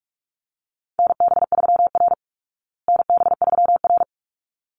Morse code, heard as a keyed tone, about 44 wpm, 700 Hz. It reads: NB3R NB3R
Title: Call Signs Repeated 3x - 45wpm